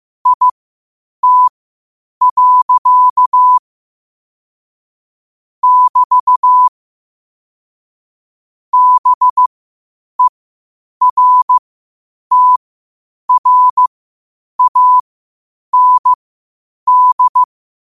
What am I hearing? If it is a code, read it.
Morse code, heard as a keyed tone, about 15 wpm, spaced as for 8 wpm, 1000 Hz. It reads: IT. <BT> BERTRAND
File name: qotdfiles / 8wpm